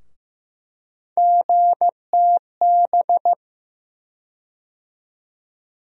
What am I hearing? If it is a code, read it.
Morse code, heard as a keyed tone, about 15 wpm, 700 Hz. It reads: GTB
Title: USA Call Sign Postfixes - 3 Letters 15wpm